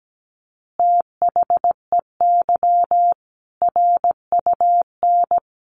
Morse code, heard as a keyed tone, about 17 wpm, 700 Hz. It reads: THEY RUN
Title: Easy Phrases 17wpm